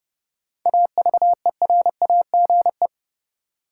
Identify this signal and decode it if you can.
Morse code, heard as a keyed tone, about 30 wpm, 700 Hz. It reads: AVERAGE
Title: Sentences from Top 1400 Words - New Word Drill Alt - 30wpm